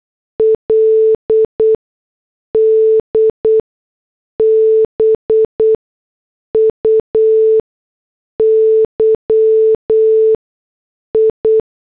Morse code, heard as a keyed tone, about 8 wpm, spaced as for 6 wpm, 440 Hz. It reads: LD BUYI